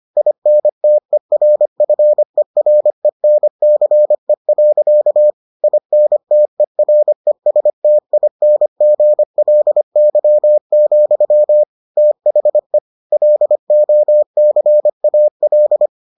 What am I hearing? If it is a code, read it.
Morse code, heard as a keyed tone, about 25 wpm, 600 Hz. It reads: INTERFERENCE. INTERESTINGLY, THE LOCAL